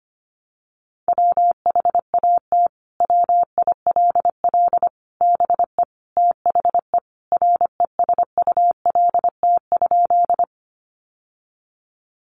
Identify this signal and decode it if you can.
Morse code, heard as a keyed tone, about 25 wpm, 700 Hz. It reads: WHAT WILL BE THE RESULT?
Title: Sentences from Top 700 Words 25wpm